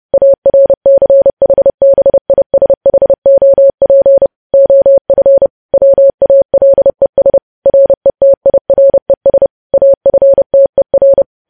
Morse code, heard as a keyed tone, about 30 wpm, 550 Hz. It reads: ARCHBISHOP OF WALES RETIRES AFTER
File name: podcast / News headlines-1x-30x30